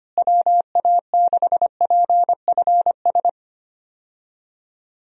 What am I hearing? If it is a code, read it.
Morse code, heard as a keyed tone, about 25 wpm, 700 Hz. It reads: WA6PFS
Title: USA Call Signs - 6 Characters - Repeated 2x - 25wpm